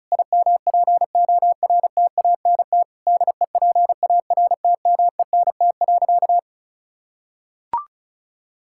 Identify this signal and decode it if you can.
Morse code, heard as a keyed tone, about 34 wpm, 700 Hz. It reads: IMPORTANT DEPARTMENT.
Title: Sentences from Top 1000 Words - Review Part 5 - 35wpm